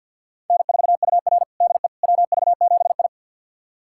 Morse code, heard as a keyed tone, about 51 wpm, 700 Hz. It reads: N4UR DE WV7I
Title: USA Callsign DE Callsign - 4 Characters - 50wpm